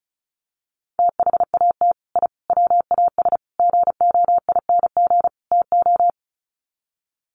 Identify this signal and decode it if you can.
Morse code, heard as a keyed tone, about 34 wpm, 700 Hz. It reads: THAT I WAS GOING TO
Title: Sets of 5 Words Encore 35wpm